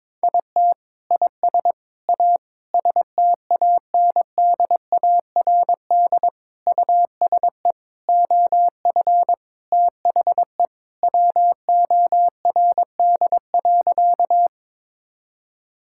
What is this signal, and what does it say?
Morse code, heard as a keyed tone, about 22 wpm, 700 Hz. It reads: IT IS A STANDARD USE OF THE WORD.
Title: Sentences from Top 1000 Words - Review Part 6 - 22wpm